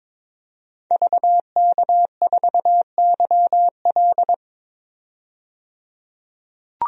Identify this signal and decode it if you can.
Morse code, heard as a keyed tone, about 22 wpm, 700 Hz. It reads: VK4YL
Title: DXCC-100 Call Signs - 5 Characters - 22wpm